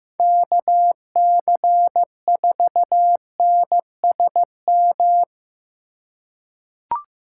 Morse code, heard as a keyed tone, about 15 wpm, 700 Hz. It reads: KC4NSM
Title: USA Call Signs - 6 Characters - 2x Character Spacing - 15wpm